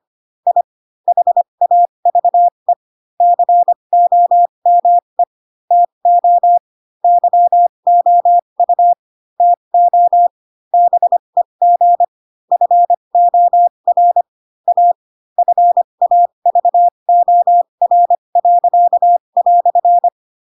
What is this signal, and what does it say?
Morse code, heard as a keyed tone, about 25 wpm, spaced as for 22 wpm, 700 Hz. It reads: I HAVE COME TO YOU TO BEG FOR A FAVOR."
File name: Pinocchio / Pinocchio ch2 18 25 ik1hge